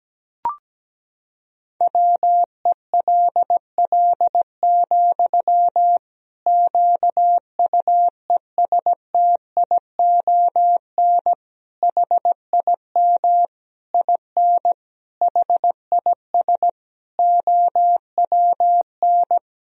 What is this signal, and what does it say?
Morse code, heard as a keyed tone, about 17 wpm, 700 Hz. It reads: WELL, QUESTION HIM IN HIS OWN
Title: Sentences from Top 1000 Words - Review Part 3 - 17wpm